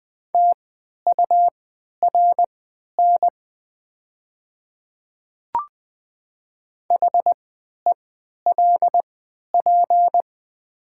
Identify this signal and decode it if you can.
Morse code, heard as a keyed tone, about 20 wpm, spaced as for 11 wpm, 700 Hz. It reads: TURN HELP
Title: 3x Character Spacing - Top 200 Words - 20wpm